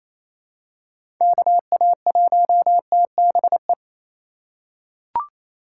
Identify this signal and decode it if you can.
Morse code, heard as a keyed tone, about 28 wpm, 700 Hz. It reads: KA1TBE
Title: USA Call Signs - 6 Characters - 2x Character Spacing - 28wpm